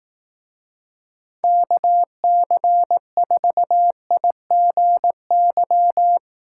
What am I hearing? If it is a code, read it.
Morse code, heard as a keyed tone, about 18 wpm, 700 Hz. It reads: KC4IGY
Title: Speed Racing - USA Call Signs 15wpm